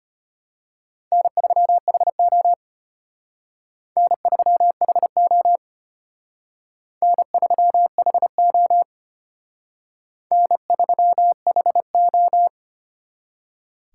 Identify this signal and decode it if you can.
Morse code, heard as a keyed tone, about 31 wpm, 700 Hz. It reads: N3HO N3HO N3HO N3HO
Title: Speed Racing - USA 1x2 Call Signs 25wpm